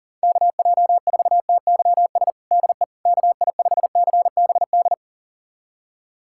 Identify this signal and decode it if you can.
Morse code, heard as a keyed tone, about 40 wpm, 700 Hz. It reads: KJ4TYS DE KI5CBD
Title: USA Callsign DE Callsign - 6 Characters - 40wpm